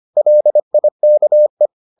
Morse code, heard as a keyed tone, about 25 wpm, 600 Hz. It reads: LIKE